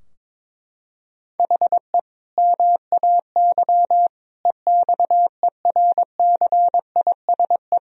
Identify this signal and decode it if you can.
Morse code, heard as a keyed tone, about 22 wpm, 700 Hz. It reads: HE MAY EXERCISE
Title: Sentences from Top 1000 Words - Review Part 4 - 22wpm